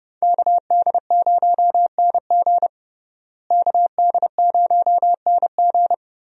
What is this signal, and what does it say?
Morse code, heard as a keyed tone, about 30 wpm, 700 Hz. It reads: KD0NG KD0NG
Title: USA Call Signs - 5 Characters - Repeated 3x - 30wpm